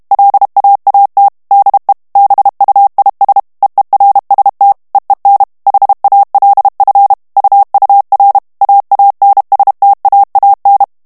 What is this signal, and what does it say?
Morse code, heard as a keyed tone, about 32 wpm, 800 Hz. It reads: LAAT DE BUIS EERST EEN HALF UUR AANSTAAN
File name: HOLLANDAIS texte clair 2 vitesse 30 mots minute